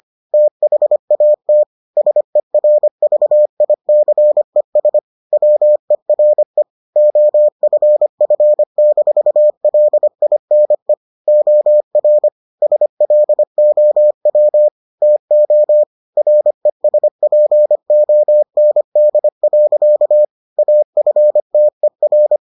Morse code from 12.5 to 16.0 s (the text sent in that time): SLOW TO